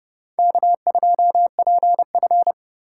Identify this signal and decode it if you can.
Morse code, heard as a keyed tone, about 38 wpm, 700 Hz. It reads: K2PF